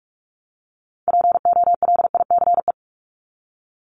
Morse code, heard as a keyed tone, about 44 wpm, 700 Hz. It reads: POLICE
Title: Top 1751-2000 Words 45wpm